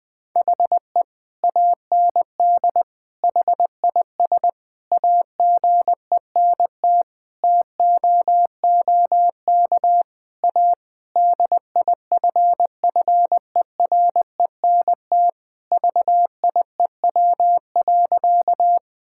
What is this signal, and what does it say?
Morse code, heard as a keyed tone, about 20 wpm, 700 Hz. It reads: HE AND HIS AGENT TOOK A DIFFERENT VIEW.
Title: Sentences from Top 1600 Words 20wpm